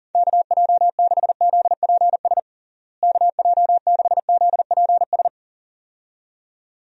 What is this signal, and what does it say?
Morse code, heard as a keyed tone, about 40 wpm, 700 Hz. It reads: KJ6ZPS KJ6ZPS
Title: USA Call Signs - 6 Characters - Repeated 2x - 40wpm